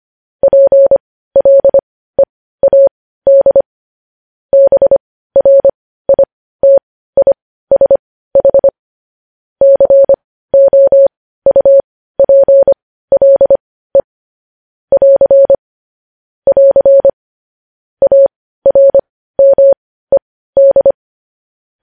Morse code, heard as a keyed tone, about 25 wpm, spaced as for 15 wpm, 550 Hz. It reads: PLEAD BRITISH COUPLE <AR> <AR> ARMED
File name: podcast / News headlines-1x-25x15